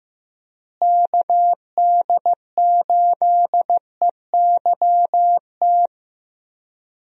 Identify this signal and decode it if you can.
Morse code, heard as a keyed tone, about 15 wpm, 700 Hz. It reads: KD8EYT
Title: USA Call Signs - 6 Characters - Repeated 3x - 15wpm